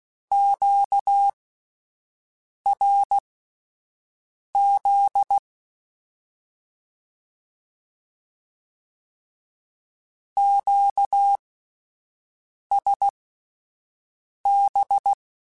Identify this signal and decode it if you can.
Morse code, heard as a keyed tone, about 16 wpm, spaced as for 5 wpm, 800 Hz. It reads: QRZ QSB
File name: (spaced) / Q-Calls Spaced-mp3s / Q Calls Twice sp @10